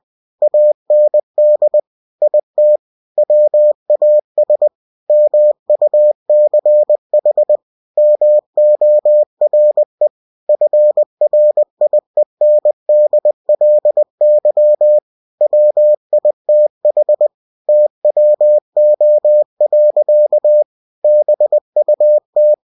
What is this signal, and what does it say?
Morse code, heard as a keyed tone, about 20 wpm, 600 Hz. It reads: AND IT WAS MUCH MORE FRIENDLY WITH TWO. BUT
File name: cw-books / winnie-the-pooh / winnie-20wpm-page-0041